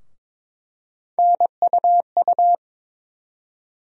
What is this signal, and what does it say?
Morse code, heard as a keyed tone, about 21 wpm, 700 Hz. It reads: NUU
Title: USA Call Sign Postfixes - 3 Letters 22wpm